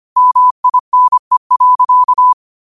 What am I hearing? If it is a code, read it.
Morse code, heard as a keyed tone, about 25 wpm, 1000 Hz. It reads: MINE.
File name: qotdfiles / 25wpm